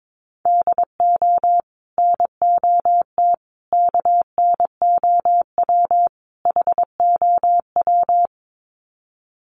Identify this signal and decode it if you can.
Morse code, heard as a keyed tone, about 22 wpm, 700 Hz. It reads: DO NOT KNOW HOW
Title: Sets of 4 Words 22wpm